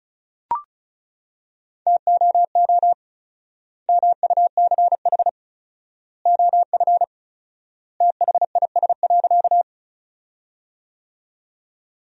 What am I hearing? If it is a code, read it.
Morse code, heard as a keyed tone, about 36 wpm, 700 Hz. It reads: TOO MUCH OF THIS.
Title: 4x Word-Spacing - Sentences from Top 200 Words - 35wpm